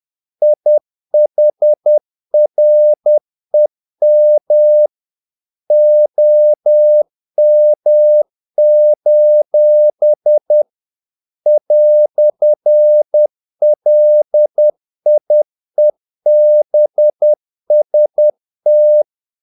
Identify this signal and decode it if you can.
Morse code, heard as a keyed tone, about 10 wpm, 600 Hz. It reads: IHREM OM: "LIEBST
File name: dk5ke / Humor-50BpM